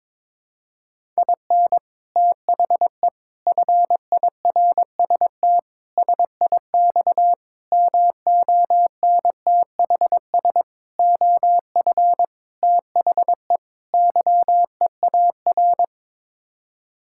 Morse code, heard as a keyed tone, about 22 wpm, 700 Hz. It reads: IN THE FIRST SIX MONTHS OF THE YEAR
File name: practice / 22wpm / Sets of 8 Words - 22wpm